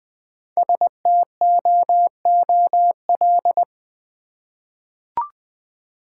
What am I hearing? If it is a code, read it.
Morse code, heard as a keyed tone, about 20 wpm, 700 Hz. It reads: STOOL
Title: Easy Words with Hints - 20wpm